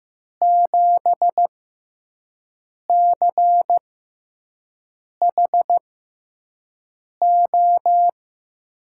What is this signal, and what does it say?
Morse code, heard as a keyed tone, about 15 wpm, spaced as for 5 wpm, 700 Hz. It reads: 7CHO
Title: USA Call Signs - 6 Characters - 6x Character Spacing - 15wpm